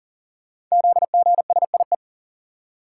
Morse code, heard as a keyed tone, about 43 wpm, 690 Hz. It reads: ZGSIE
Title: Code Groups - 5 Letters - 40wpm